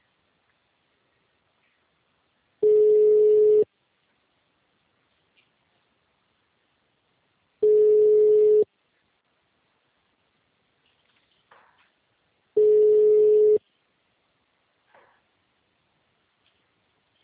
Ringback tone (Europe, 425 Hz).
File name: out- -2031- - - .13816